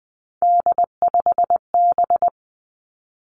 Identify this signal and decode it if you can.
Morse code, heard as a keyed tone, about 20 wpm, 700 Hz. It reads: D5B
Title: Three Challenging Letter-Number Combinations 20wpm